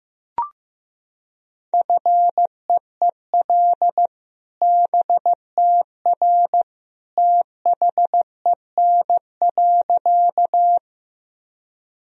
Morse code, heard as a keyed tone, about 15 wpm, 700 Hz. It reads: FEEL BTR THEN.
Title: Sentences from Top 400 Words and Abbreviations 15wpm